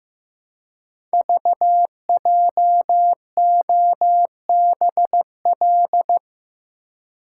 Morse code, heard as a keyed tone, about 15 wpm, 700 Hz. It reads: VJOBL
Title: Code Groups - 5 Letters - 15wpm